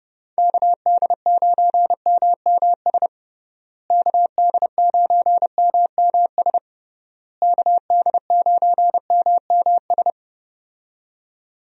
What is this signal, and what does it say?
Morse code, heard as a keyed tone, about 30 wpm, 700 Hz. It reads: KD9MMS KD9MMS KD9MMS
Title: USA Call Signs - 6 Characters - Repeated 3x - 30wpm